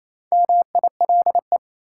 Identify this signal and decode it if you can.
Morse code, heard as a keyed tone, about 28 wpm, 700 Hz. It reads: MILE
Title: Sentences from Top 400 Words - New Word Drill - 28wpm